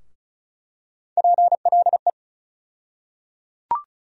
Morse code, heard as a keyed tone, about 34 wpm, 700 Hz. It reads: PLE
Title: USA Call Sign Postfixes - 3 Letters 35wpm